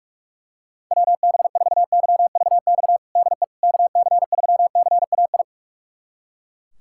Morse code, heard as a keyed tone, about 41 wpm, 700 Hz. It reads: WB4YVX DE KC3CAI
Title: USA Callsign DE Callsign - 6 Characters - 45wpm